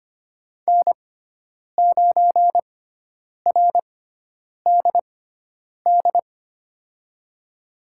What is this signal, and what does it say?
Morse code, heard as a keyed tone, about 25 wpm, spaced as for 9 wpm, 700 Hz. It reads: N9RDD
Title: USA Call Signs - 5 Characters - 6x Character Spacing - 25wpm